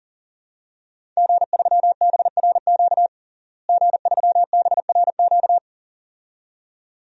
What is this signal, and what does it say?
Morse code, heard as a keyed tone, about 40 wpm, 700 Hz. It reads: G3BRQ G3BRQ
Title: DXCC-100 Call Signs - 5 Characters - Repeated 2x - 40wpm